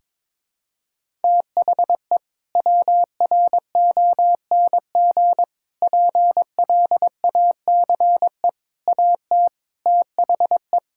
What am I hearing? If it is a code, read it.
Morse code, heard as a keyed tone, about 23 wpm, 700 Hz. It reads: THE WRONG PLACE AT THE
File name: practice / 22wpm / Sets of 7 Words - 22wpm